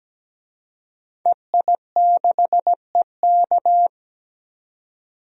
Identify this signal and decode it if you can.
Morse code, heard as a keyed tone, about 17 wpm, 700 Hz. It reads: EI6EK
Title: DXCC-100 Call Signs - 5 Characters - 17wpm